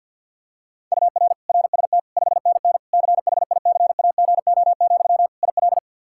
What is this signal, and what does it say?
Morse code, heard as a keyed tone, about 48 wpm, 700 Hz. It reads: UR RST 5NN CHICAGO, IL
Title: K1USN SST QSO Practice 50wpm